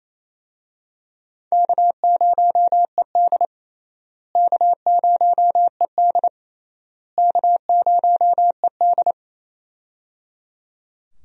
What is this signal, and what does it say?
Morse code, heard as a keyed tone, about 27 wpm, 700 Hz. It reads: K0ED K0ED K0ED
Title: USA Call Signs - 4 Characters - Repeated 3x - 28wpm